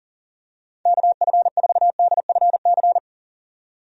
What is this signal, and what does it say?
Morse code, heard as a keyed tone, about 37 wpm, 700 Hz. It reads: KF4DFC
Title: USA Call Signs - 6 Characters - 40wpm